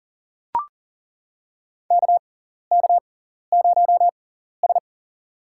Morse code, heard as a keyed tone, about 39 wpm, spaced as for 14 wpm, 700 Hz. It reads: KK0S
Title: USA Call Signs - 4 Characters - 6x Character Spacing - 40wpm